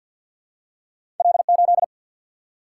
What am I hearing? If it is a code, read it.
Morse code, heard as a keyed tone, about 50 wpm, 700 Hz. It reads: L 8
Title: Single-Letter Single-Number 50wpm